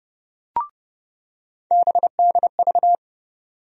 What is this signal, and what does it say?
Morse code, heard as a keyed tone, about 30 wpm, 700 Hz. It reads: BDV
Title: USA Call Sign Postfixes - 3 Letters 30wpm